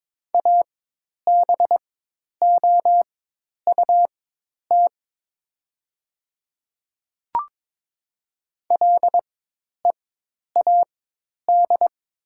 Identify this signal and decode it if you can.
Morse code, heard as a keyed tone, about 22 wpm, spaced as for 10 wpm, 700 Hz. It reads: ABOUT LEAD
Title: 4x Character Spacing - Top 500 Words - 22wpm